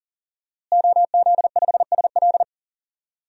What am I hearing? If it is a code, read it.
Morse code, heard as a keyed tone, about 38 wpm, 700 Hz. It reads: OZ5SL
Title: DXCC-100 Call Signs - 5 Characters - Repeated 2x - 40wpm